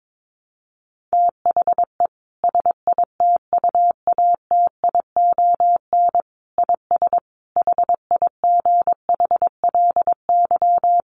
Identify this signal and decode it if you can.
Morse code, heard as a keyed tone, about 22 wpm, 700 Hz. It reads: THE SITUATION IS HIGHLY
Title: Sentences from Top 1300 Words 22wpm